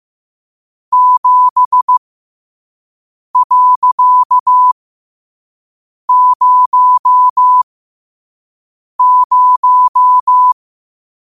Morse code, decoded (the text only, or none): 7.00